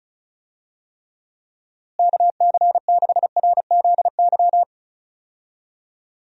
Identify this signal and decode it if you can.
Morse code, heard as a keyed tone, about 36 wpm, 700 Hz. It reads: KC6RZY
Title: USA Call Signs - 6 Characters - Repeated 3x - 35wpm